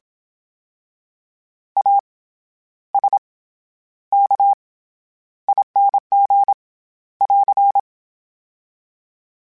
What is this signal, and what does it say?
Morse code, heard as a keyed tone, about 27 wpm, 800 Hz. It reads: A S K ING <AR>
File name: Special INGs-mp3s / Special INGs-mp3s / SlowToFaster ing @25